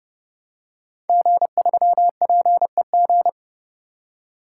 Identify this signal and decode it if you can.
Morse code, heard as a keyed tone, about 30 wpm, 700 Hz. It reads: G3PEG
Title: Code Groups - 5 Letter:Numbers - 30wpm